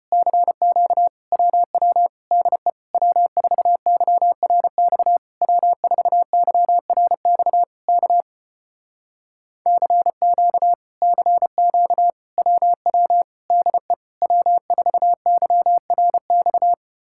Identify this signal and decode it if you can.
Morse code, heard as a keyed tone, about 33 wpm, 700 Hz. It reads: CQ WW DE W4YRX W4YRX K CQ CQ WW DE W4YRX